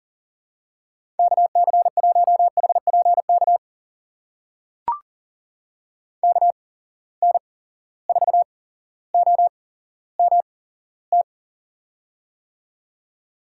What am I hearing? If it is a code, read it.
Morse code, heard as a keyed tone, about 40 wpm, spaced as for 17 wpm, 700 Hz. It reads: KC1HPK KN4OMT
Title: USA Call Signs - 6 Characters - 8x Character Spacing - 40wpm